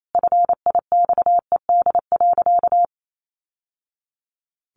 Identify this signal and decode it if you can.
Morse code, heard as a keyed tone, about 28 wpm, 700 Hz. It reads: FIXED.
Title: Sentences from Top 1100 Words 28wpm